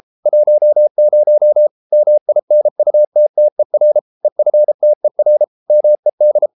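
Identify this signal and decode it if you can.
Morse code, heard as a keyed tone, about 33 wpm, 600 Hz. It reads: 10 MINUTTER EFTER MED
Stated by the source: Morse code at 33 wpm, QRQ.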